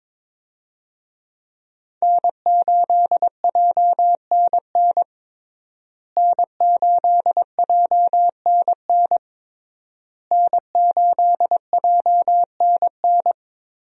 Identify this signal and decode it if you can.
Morse code, heard as a keyed tone, about 22 wpm, 700 Hz. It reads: N8JNN N8JNN N8JNN
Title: USA Call Signs - 5 Characters - Repeated 3x - 22wpm